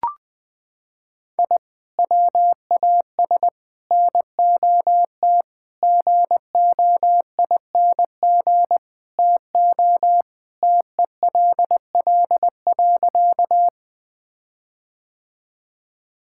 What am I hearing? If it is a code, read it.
Morse code, heard as a keyed tone, about 20 wpm, 700 Hz. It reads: I WAS NOT GOING TO TELL.
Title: Sentences with 5-Word Phrases - 20wpm